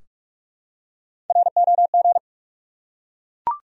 Morse code, heard as a keyed tone, about 44 wpm, 700 Hz. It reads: ROG
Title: USA Common Names - Max 5 Letters - 45wpm